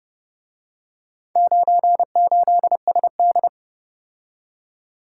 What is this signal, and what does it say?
Morse code, heard as a keyed tone, about 29 wpm, 700 Hz. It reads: 98SD